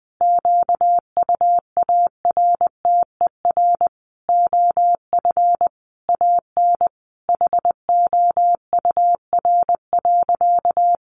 Morse code, heard as a keyed tone, about 20 wpm, 700 Hz. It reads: QUARTER OF AN HOUR.
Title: Sentences from Top 1300 Words 20wpm